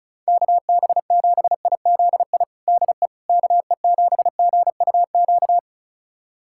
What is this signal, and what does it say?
Morse code, heard as a keyed tone, about 36 wpm, 700 Hz. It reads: KB7IZI DE KE7GUQ
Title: USA Callsign DE Callsign - 6 Characters - 35wpm